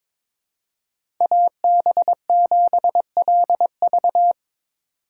Morse code, heard as a keyed tone, about 22 wpm, 700 Hz. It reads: AB7LV
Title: USA Call Signs - 8x Character Spacing - 22wpm